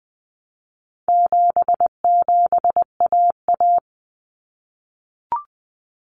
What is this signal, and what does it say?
Morse code, heard as a keyed tone, about 20 wpm, 700 Hz. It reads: 77AA
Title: Common Cut Numbers - 4-Digit 20wpm